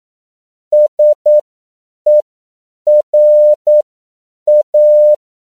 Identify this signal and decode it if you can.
Morse code, heard as a keyed tone, about 9 wpm, spaced as for 7 wpm, 600 Hz. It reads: SERA
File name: CW / 10PPM